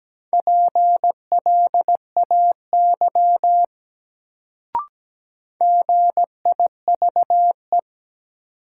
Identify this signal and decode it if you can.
Morse code, heard as a keyed tone, about 17 wpm, 700 Hz. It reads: PLAY GIVE
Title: Sentences from Top 200 Words - New Word Drill - 17wpm